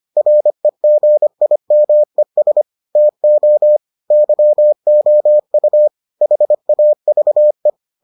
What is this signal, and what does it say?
Morse code, heard as a keyed tone, about 25 wpm, 600 Hz. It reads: REGIMES TO YOU HAVE